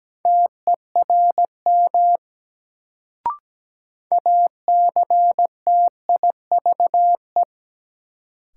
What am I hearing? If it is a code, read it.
Morse code, heard as a keyed tone, about 17 wpm, 700 Hz. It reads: TERM ACTIVE
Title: Sentences from Top 900 Words - New Word Drill - 17wpm